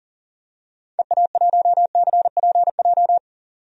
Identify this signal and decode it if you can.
Morse code, heard as a keyed tone, about 40 wpm, 700 Hz. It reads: EA1CPJ
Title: DXCC-100 Call Signs - 6 Characters - 40wpm